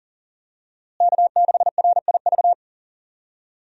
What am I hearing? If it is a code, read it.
Morse code, heard as a keyed tone, about 40 wpm, 700 Hz. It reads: K6RIV
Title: USA Call Signs - 5 Characters - Repeated 3x - 40wpm